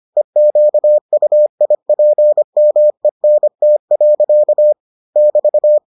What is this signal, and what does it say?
Morse code, heard as a keyed tone, about 25 wpm, 600 Hz. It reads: EQUIPMENT. <BT>